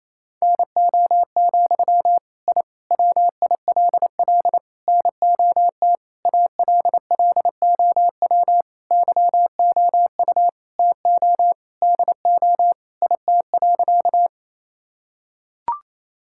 Morse code, heard as a keyed tone, about 28 wpm, 700 Hz. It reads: NO, I WILL NOT ALLOW YOU TO DO IT.
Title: Sentences from Top 1000 Words - Review Part 5 - 28wpm